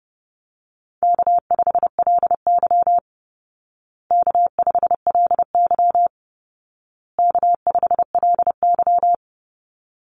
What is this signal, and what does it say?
Morse code, heard as a keyed tone, about 30 wpm, 700 Hz. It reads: K5LY K5LY K5LY
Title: Call Signs Repeated 2x - 30wpm